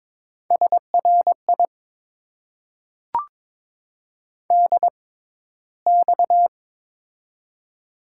Morse code, heard as a keyed tone, about 22 wpm, spaced as for 13 wpm, 700 Hz. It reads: SRI DX
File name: Top 90 QSO Elements - 6x Character Spacing - 22wpm